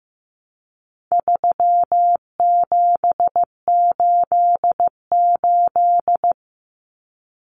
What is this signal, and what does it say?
Morse code, heard as a keyed tone, about 15 wpm, 700 Hz. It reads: 3788
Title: Common Cut Numbers - 4-Digit 15wpm